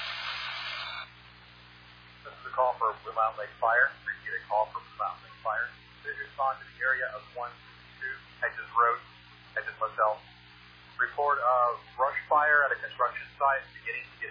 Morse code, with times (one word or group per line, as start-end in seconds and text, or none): none